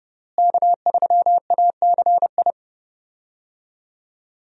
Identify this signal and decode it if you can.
Morse code, heard as a keyed tone, about 30 wpm, 700 Hz. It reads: K3ACI
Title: USA Call Signs - 5 Characters - Repeated 2x - 30wpm